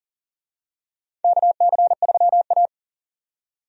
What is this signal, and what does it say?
Morse code, heard as a keyed tone, about 40 wpm, 700 Hz. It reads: KC3A
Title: USA Call Signs - 4 Characters - 40wpm